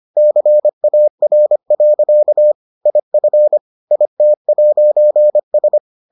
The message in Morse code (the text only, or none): CAR. IF IT'S